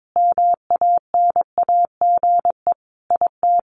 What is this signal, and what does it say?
Morse code, heard as a keyed tone, about 22 wpm, 700 Hz. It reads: MANAGE IT
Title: Sentences from Top 1400 Words 22wpm